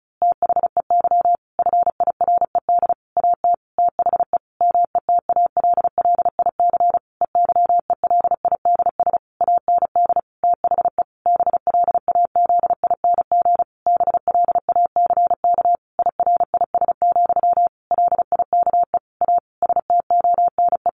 Morse code, heard as a keyed tone, about 34 wpm, 700 Hz. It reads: THEY FIRED AT THE METALLIC EYELIDS AND THE BLAZING BLACK IRIS, LIKE A STONE
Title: A Sound Of Thunder 35wpm